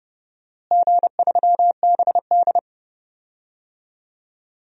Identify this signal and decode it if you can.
Morse code, heard as a keyed tone, about 30 wpm, 700 Hz. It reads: G3BD
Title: DXCC-100 Call Signs - 4 Characters - Repeated 2x - 30wpm